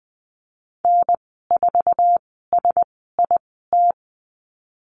Morse code, heard as a keyed tone, about 20 wpm, spaced as for 15 wpm, 700 Hz. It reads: N4SIT
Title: 055 Lesson 10g - Letter r - Call Signs 20wpm